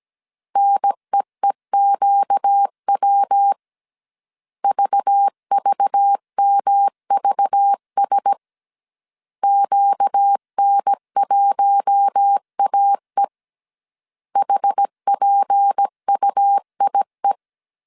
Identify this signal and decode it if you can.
Morse code, heard as a keyed tone, about 17 wpm, 800 Hz. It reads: NEEQW VVMVS QN1AE HPUIE
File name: 16wpm